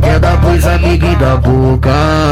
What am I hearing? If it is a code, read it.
There is no code.